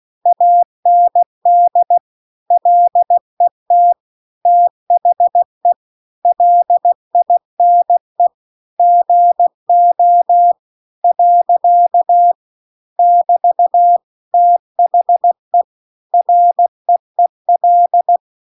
Morse code, heard as a keyed tone, about 16 wpm, 700 Hz. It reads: AND LET THE LINE GO . <BT> THE REEL